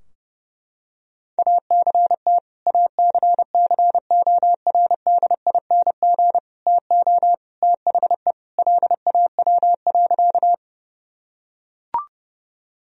Morse code, heard as a keyed tone, about 30 wpm, 700 Hz. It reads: ACT ACCORDING TO THE LAW.
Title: Sentences from Top 1000 Words - Review Part 4 - 30wpm